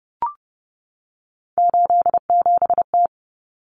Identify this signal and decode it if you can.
Morse code, heard as a keyed tone, about 30 wpm, 700 Hz. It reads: 87T
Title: Common Cut Numbers - 3-Digit 30wpm